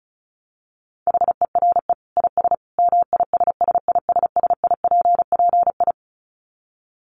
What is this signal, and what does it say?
Morse code, heard as a keyed tone, about 34 wpm, 700 Hz. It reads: HERE IS MISSISSIPPI
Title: Dit Challenge 35wpm